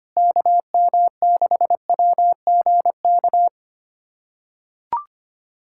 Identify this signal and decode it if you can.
Morse code, heard as a keyed tone, about 25 wpm, 700 Hz. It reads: KM6WGK
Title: USA Callsign DE Callsign 25wpm